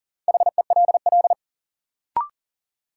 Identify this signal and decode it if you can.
Morse code, heard as a keyed tone, about 39 wpm, 700 Hz. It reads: HELL
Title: Sets of 5 Words Encore - New Word Drill - 40wpm